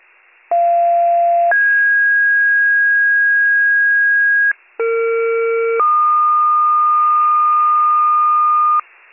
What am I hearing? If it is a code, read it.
Two-tone page, heard as a one-second tone, then a three-second tone, about 690 then 1650 Hz, about 460 then 1220 Hz, with noise. It is heard from 0.5 to 8.8 s.